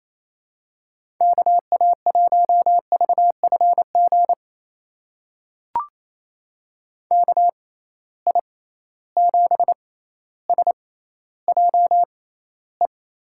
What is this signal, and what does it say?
Morse code, heard as a keyed tone, about 28 wpm, spaced as for 14 wpm, 700 Hz. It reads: KA1VFG KI7SJE
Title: USA Call Signs - 6x Character Spacing - 28wpm